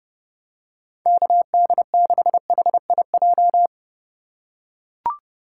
Morse code, heard as a keyed tone, about 32 wpm, 700 Hz. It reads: KD6HIJ
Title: USA Call Signs - 6 Characters - 2x Character Spacing - 30wpm